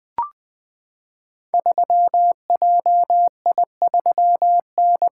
Morse code, heard as a keyed tone, about 20 wpm, 700 Hz. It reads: 3JI3N
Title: Code Groups - 5 Letter:Numbers - 20wpm